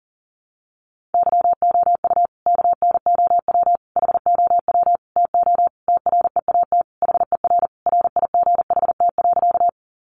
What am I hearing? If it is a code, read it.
Morse code, heard as a keyed tone, about 40 wpm, 700 Hz. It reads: YOU KNOW HOW TO TREAT HER RIGHT.
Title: Sentences from Top 1300 Words 40wpm